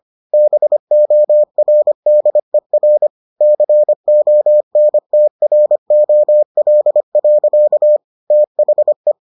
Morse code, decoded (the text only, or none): BORDER CONTROL. THE